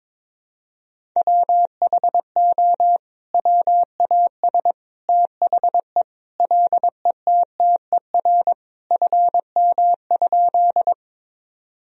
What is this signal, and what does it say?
Morse code, heard as a keyed tone, about 22 wpm, 700 Hz. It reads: WHO WAS THE LETTER FM?
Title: Sentences from Top 300 Words and Abbreviations 22wpm